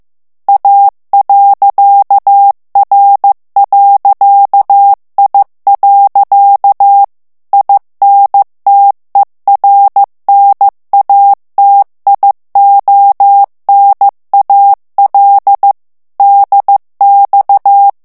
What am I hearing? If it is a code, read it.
Morse code, heard as a keyed tone, about 15 wpm, 800 Hz. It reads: A.R.I. INTERNATIONAL DX
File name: morse mp3 / ITALIEN texte clair 2 vitesse 14 mots minute